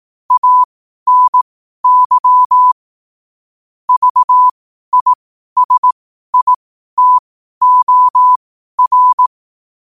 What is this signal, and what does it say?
Morse code, heard as a keyed tone, about 18 wpm, spaced as for 12 wpm, 1000 Hz. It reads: ANY VISITOR
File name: qotdfiles / 12wpm